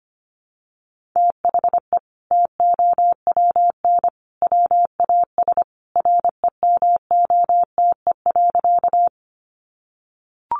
Morse code, heard as a keyed tone, about 25 wpm, 700 Hz. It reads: THE TOWN WAS REMOTE.
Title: Sentences from Top 1300 Words 25wpm